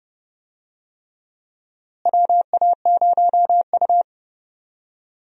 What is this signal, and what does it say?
Morse code, heard as a keyed tone, about 30 wpm, 700 Hz. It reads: WA0U
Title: USA Call Signs - 4 Characters - Repeated 3x - 30wpm